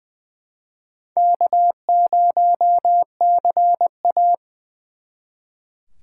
Morse code, heard as a keyed tone, about 25 wpm, 700 Hz. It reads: K0CA
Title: Speed Racing - USA 1x2 Call Signs 20wpm